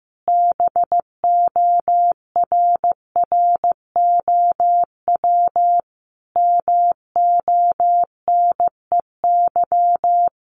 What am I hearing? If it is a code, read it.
Morse code, heard as a keyed tone, about 15 wpm, 700 Hz. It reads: BORROW MONEY
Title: Sentences from Top 1500 Words 15wpm